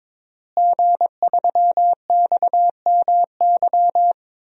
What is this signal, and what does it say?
Morse code, heard as a keyed tone, about 21 wpm, 700 Hz. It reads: G3XMY
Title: DXCC-100 Call Signs - 5 Characters - Repeated 2x - 22wpm